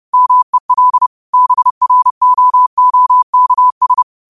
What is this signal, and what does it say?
Morse code, heard as a keyed tone, about 30 wpm, 1000 Hz. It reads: MEL BROOKS
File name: qotdfiles / 30wpm